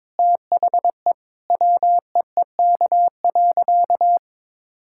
Morse code, heard as a keyed tone, about 22 wpm, 700 Hz. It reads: THE WEEK.
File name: practice / 22wpm / Sentences with 4-Word Phrases - 22wpm